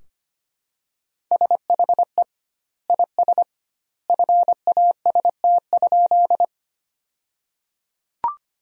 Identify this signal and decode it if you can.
Morse code, heard as a keyed tone, about 25 wpm, 700 Hz. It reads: SHE IS FAST?
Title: 2x Word-Spacing - Sentences from Top 400 Words - 25wpm